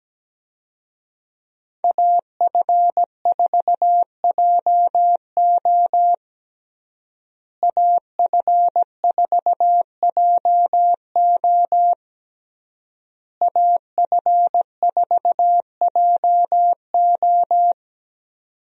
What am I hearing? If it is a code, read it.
Morse code, heard as a keyed tone, about 17 wpm, 700 Hz. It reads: AF4JO AF4JO AF4JO
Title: USA Call Signs - 5 Characters - Repeated 3x - 17wpm